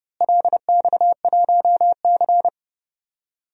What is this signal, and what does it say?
Morse code, heard as a keyed tone, about 31 wpm, 700 Hz. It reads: LX1C
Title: DXCC-100 Call Signs - 4 Characters - Repeated 3x - 30wpm